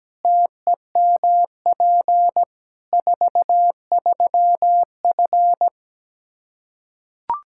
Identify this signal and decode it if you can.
Morse code, heard as a keyed tone, about 17 wpm, 700 Hz. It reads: TEMP 43F
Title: Practice QSO - Exchange 2 - Info - 17wpm